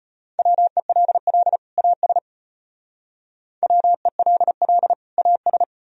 Morse code, heard as a keyed tone, about 32 wpm, 700 Hz. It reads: WELL AS WELL AS